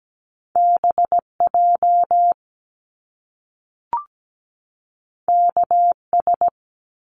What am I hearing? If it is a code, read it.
Morse code, heard as a keyed tone, about 17 wpm, 700 Hz. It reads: BJ KS
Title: Top Two Letter Combinations in Words 17wpm